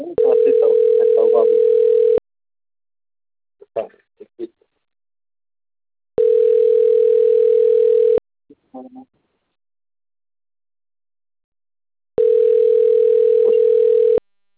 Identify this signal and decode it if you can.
Ringback tone (North America), 440 + 480 Hz.